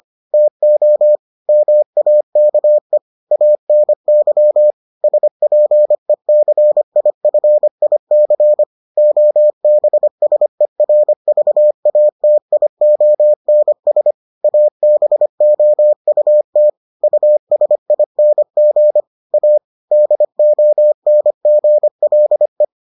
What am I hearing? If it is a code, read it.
Morse code, heard as a keyed tone, about 25 wpm, 600 Hz. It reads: TO MAKE ANY SPECIFIC OBSERVATIONS ABOUT USING A DONGLE